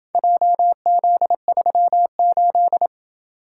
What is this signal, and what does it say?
Morse code, heard as a keyed tone, about 27 wpm, 700 Hz. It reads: JZ38